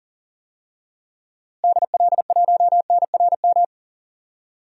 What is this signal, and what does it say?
Morse code, heard as a keyed tone, about 40 wpm, 700 Hz. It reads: DL1NRM
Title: DXCC-100 Call Signs - 6 Characters - Repeated 2x - 40wpm